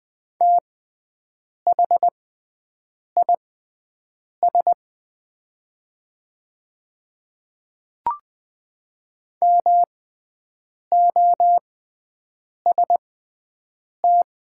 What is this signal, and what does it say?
Morse code, heard as a keyed tone, about 20 wpm, spaced as for 7 wpm, 700 Hz. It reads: THIS MOST
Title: 6x Character Spacing - Top 100 Words - 20wpm